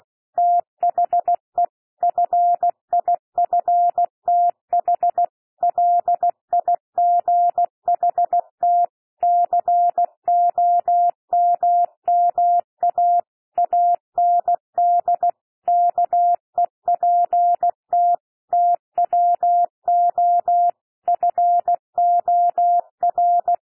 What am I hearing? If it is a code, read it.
Morse code, heard as a keyed tone, about 16 wpm, 700 Hz. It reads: THE FIFTH LIGHT COMMA AND KEPT TWO FOR
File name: cwmp3 / P1 SOUND PASSAGE 16 wpm